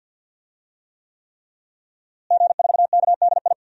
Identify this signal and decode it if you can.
Morse code, heard as a keyed tone, about 50 wpm, 700 Hz. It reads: G4KDI
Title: DXCC-100 Call Signs - 5 Characters - Repeated 3x - 50wpm